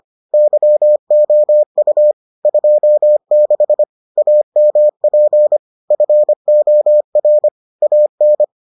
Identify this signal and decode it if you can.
Morse code, heard as a keyed tone, about 25 wpm, 600 Hz. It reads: YOU 26 AMP FOR AN